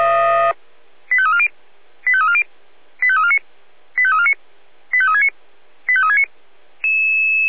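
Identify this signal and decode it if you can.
Five-tone selective call (ZVEI), heard as quick runs of five tones, tones of about 80 ms. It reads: 85340 85340 85349 85349 85369 85369
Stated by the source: ZVEI five-tone call 85360, received on an SDR.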